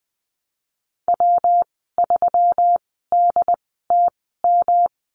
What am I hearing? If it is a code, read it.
Morse code, heard as a keyed tone, about 20 wpm, spaced as for 15 wpm, 700 Hz. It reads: W3DTM
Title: 161 Lesson 26g - Letter g - Call Signs 20wpm